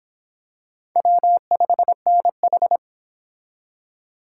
Morse code, heard as a keyed tone, about 26 wpm, 700 Hz. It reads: W5NH